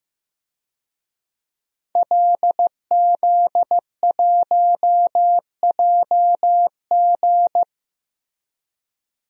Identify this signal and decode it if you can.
Morse code, heard as a keyed tone, about 15 wpm, 700 Hz. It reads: LZ1JG
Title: DXCC-100 Call Signs - 5 Characters - Repeated 2x - 15wpm